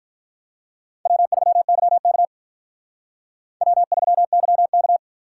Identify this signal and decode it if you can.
Morse code, heard as a keyed tone, about 44 wpm, 700 Hz. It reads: W3YK W3YK